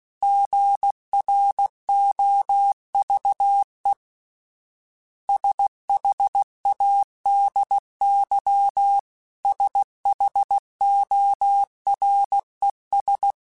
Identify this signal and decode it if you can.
Morse code, heard as a keyed tone, about 16 wpm, 800 Hz. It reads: GROVE SHADY SHORES
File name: fromSMs / FromSMs mp3s / Cities@15